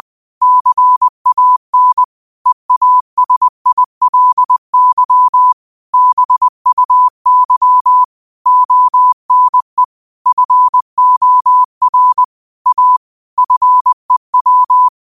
Morse code, heard as a keyed tone, about 20 wpm, 1000 Hz. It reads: CAN EASILY BUY ONE FOR A FEW
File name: qotdfiles / 20wpm